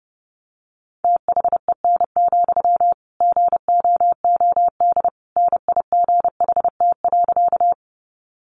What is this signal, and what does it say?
Morse code, heard as a keyed tone, about 30 wpm, 700 Hz. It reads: THEN, GOOD NIGHT.
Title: Sentences from Top 300 Words 30wpm